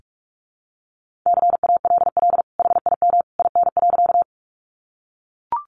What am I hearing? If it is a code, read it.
Morse code, heard as a keyed tone, about 44 wpm, 700 Hz. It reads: CALL HIM IN.
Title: Sentences from Top 100 Words 45wpm